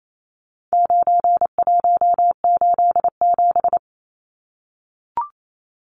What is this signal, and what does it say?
Morse code, heard as a keyed tone, about 28 wpm, 700 Hz. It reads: 9187
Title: Numbers - 4-Digit 28wpm